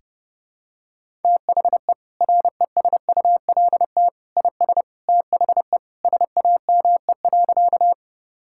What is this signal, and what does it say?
Morse code, heard as a keyed tone, about 30 wpm, 700 Hz. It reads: THE RESULT IS THE SAME.
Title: Sentences from Top 1000 Words - Review Part 5 - 30wpm